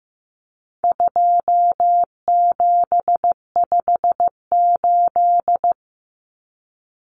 Morse code, heard as a keyed tone, about 15 wpm, 700 Hz. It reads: 2758
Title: Numbers - 4-Digit 15wpm